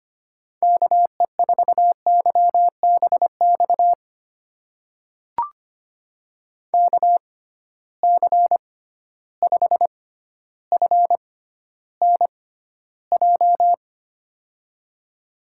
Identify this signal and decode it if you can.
Morse code, heard as a keyed tone, about 25 wpm, spaced as for 13 wpm, 700 Hz. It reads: KE4YBX KC5FNJ
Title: USA Call Signs - 6 Characters - 6x Character Spacing - 25wpm